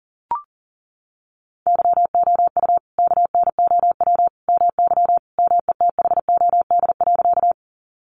Morse code, heard as a keyed tone, about 37 wpm, 700 Hz. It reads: YOU KNOW MY METHOD.
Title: Sentences from Top 600 Words 40wpm